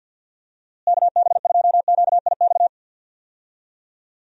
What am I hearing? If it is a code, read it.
Morse code, heard as a keyed tone, about 46 wpm, 700 Hz. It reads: KB2QIX
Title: USA Call Signs - 6 Characters - Repeated 3x - 50wpm